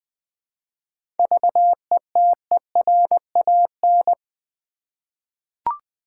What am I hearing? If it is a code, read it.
Morse code, heard as a keyed tone, about 20 wpm, 700 Hz. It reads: VETERAN
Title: 7 Letter Words - First 3 Letters Given - 20wpm